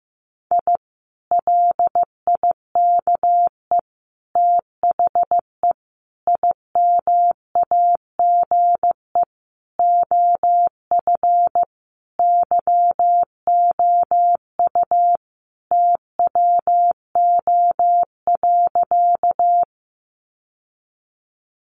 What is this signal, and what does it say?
Morse code, heard as a keyed tone, about 15 wpm, 700 Hz. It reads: I LIKE THE IMAGE OF YOU TWO.
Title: Sentences from Top 700 Words 15wpm